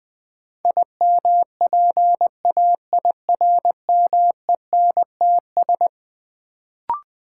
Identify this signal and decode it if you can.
Morse code, heard as a keyed tone, about 20 wpm, 700 Hz. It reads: IMPAIRMENTS
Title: Sets of 5 Words Encore - New Word Drill - 20wpm